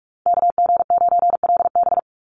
Morse code, heard as a keyed tone, about 44 wpm, 700 Hz. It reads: KG9LB
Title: Call Signs Repeated 2x - 45wpm